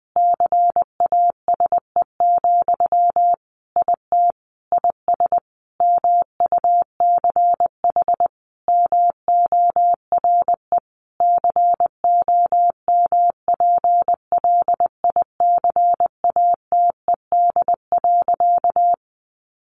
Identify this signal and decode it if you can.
Morse code, heard as a keyed tone, about 19 wpm, 700 Hz. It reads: CASE, IT IS MUCH MORE COMPLICATED.
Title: Sentences from Top 1200 Words 20wpm